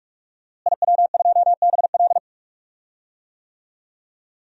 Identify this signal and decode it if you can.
Morse code, heard as a keyed tone, about 46 wpm, 700 Hz. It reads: IW2BL
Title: DXCC-100 Call Signs - 5 Characters - Repeated 3x - 45wpm